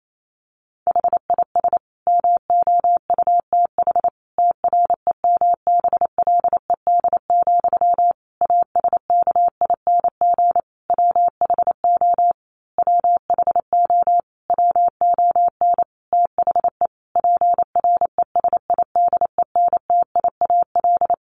Morse code, heard as a keyed tone, about 28 wpm, 700 Hz. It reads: HIS MOUTH TREMBLED, ASKING WHO WHO WON THE PRESIDENTIAL
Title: A Sound Of Thunder 28wpm